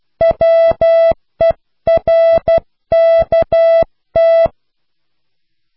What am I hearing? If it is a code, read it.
Morse code, heard as a keyed tone, about 12 wpm, 660 Hz. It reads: WERKT